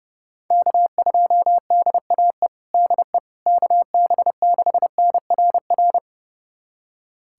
Morse code, heard as a keyed tone, about 30 wpm, 700 Hz. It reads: K2DAE DE KB6NRR
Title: USA Callsign DE Callsign 30wpm